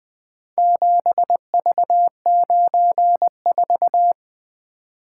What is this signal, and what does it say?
Morse code, heard as a keyed tone, about 20 wpm, 700 Hz. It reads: 7V94
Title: Four Letter-Number Combinations - 20wpm